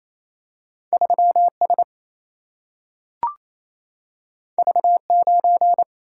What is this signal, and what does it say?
Morse code, heard as a keyed tone, about 28 wpm, 700 Hz. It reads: 3S V9
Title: Single-Letter Single-Number 28wpm